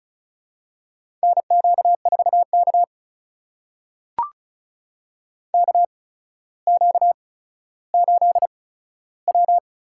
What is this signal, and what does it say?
Morse code, heard as a keyed tone, about 36 wpm, spaced as for 15 wpm, 700 Hz. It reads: NQ4K KQ8W
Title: USA Call Signs - 4 Characters - 8x Character Spacing - 35wpm